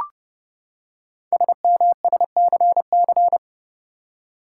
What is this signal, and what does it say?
Morse code, heard as a keyed tone, about 29 wpm, 690 Hz. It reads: SMSCC
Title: Code Groups - 5 Letters - 30wpm